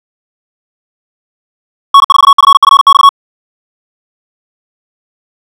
Morse code, heard as a keyed tone, about 56 wpm, 1110 Hz. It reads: N4FFX
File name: ja7uhv / SampleHST2